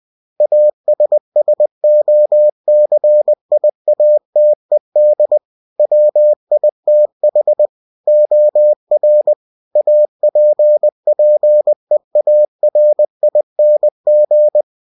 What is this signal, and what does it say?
Morse code, heard as a keyed tone, about 20 wpm, 600 Hz. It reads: ASSOCIATED WITH OR APPEARING
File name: cw-books / winnie-the-pooh / winnie-20wpm-page-0115